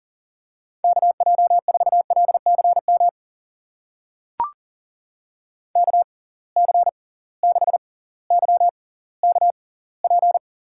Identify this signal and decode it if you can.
Morse code, heard as a keyed tone, about 40 wpm, spaced as for 20 wpm, 700 Hz. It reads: KJ4LCM KC6YKP
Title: USA Call Signs - 6x Character Spacing - 40wpm